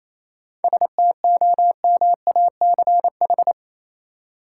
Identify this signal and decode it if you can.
Morse code, heard as a keyed tone, about 28 wpm, 700 Hz. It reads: STOMACH
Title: Seven Letter Words - 28wpm